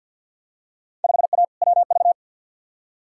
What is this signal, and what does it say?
Morse code, heard as a keyed tone, about 49 wpm, 700 Hz. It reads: 5A WV
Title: Field Day Exchanges - 50wpm